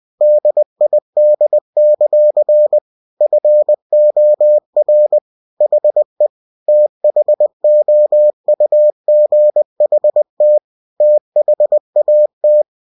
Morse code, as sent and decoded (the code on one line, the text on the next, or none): -.. .. -.. -.-.-. / ..-. --- .-. / .... / - .... --- ..- --. .... - / - .... .- -
DID; FOR HE THOUGHT THAT